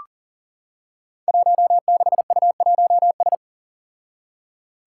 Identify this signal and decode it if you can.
Morse code, heard as a keyed tone, about 40 wpm, 700 Hz. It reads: 16U1S
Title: Five Challenging Letter-Number Combinations 40wpm